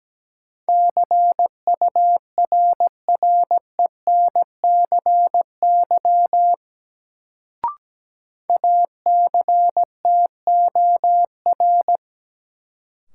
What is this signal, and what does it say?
Morse code, heard as a keyed tone, about 17 wpm, 700 Hz. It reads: CURRENCY ACTOR
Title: Sentences from Top 1500 Words - New Word Drill - 17wpm